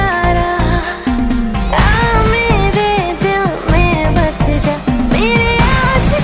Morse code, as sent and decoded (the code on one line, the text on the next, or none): none